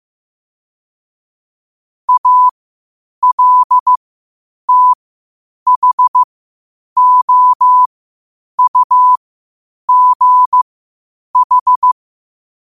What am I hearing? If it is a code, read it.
Morse code, heard as a keyed tone, about 15 wpm, spaced as for 8 wpm, 1000 Hz. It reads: ALTHOUGH
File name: qotdfiles / 8wpm